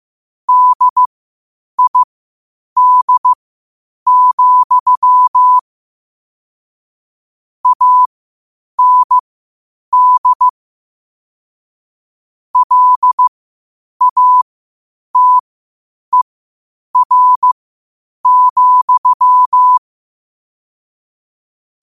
Morse code, heard as a keyed tone, about 15 wpm, spaced as for 8 wpm, 1000 Hz. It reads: DID, AND LATER,
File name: qotdfiles / 8wpm